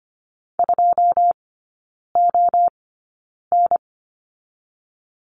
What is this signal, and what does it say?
Morse code, heard as a keyed tone, about 25 wpm, spaced as for 9 wpm, 690 Hz. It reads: 2ON
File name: Lesson 14e - Number 2 - Three Characters 25wpm